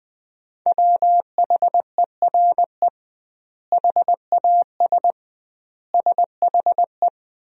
Morse code, heard as a keyed tone, about 20 wpm, 700 Hz. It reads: WHERE HAS SHE
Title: 2x Word-Spacing - Sentences from Top 300 Words - 20wpm